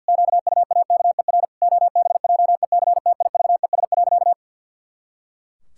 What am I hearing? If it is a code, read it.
Morse code, heard as a keyed tone, about 50 wpm, 700 Hz. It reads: QUAKER OBJECTIVES.
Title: Pangram Challenge v2 - 50wpm